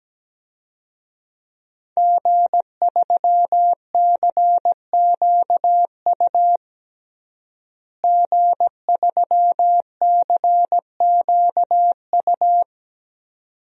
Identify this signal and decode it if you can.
Morse code, heard as a keyed tone, about 17 wpm, 700 Hz. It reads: G3CQU G3CQU
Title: DXCC-100 Call Signs - 5 Characters - Repeated 2x - 17wpm